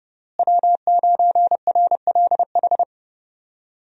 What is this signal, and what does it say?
Morse code, heard as a keyed tone, about 32 wpm, 700 Hz. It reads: W9RLH
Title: USA Call Signs - 5 Characters - Repeated 2x - 30wpm